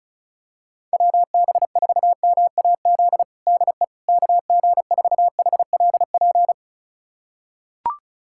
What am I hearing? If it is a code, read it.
Morse code, heard as a keyed tone, about 34 wpm, 700 Hz. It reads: WB4MAZ DE KG4HLP
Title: USA Callsign DE Callsign - 6 Characters - 35wpm